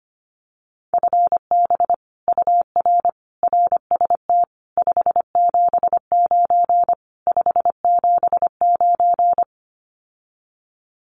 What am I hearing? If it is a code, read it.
Morse code, heard as a keyed tone, about 25 wpm, 700 Hz. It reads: FB UR RST 579 579
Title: Common QSO Phrases Encore 25wpm